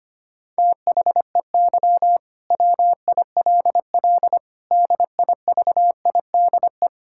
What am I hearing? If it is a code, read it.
Morse code, heard as a keyed tone, about 25 wpm, 700 Hz. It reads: THEY WILL DIVIDE
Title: Sentences from Top 1600 Words 25wpm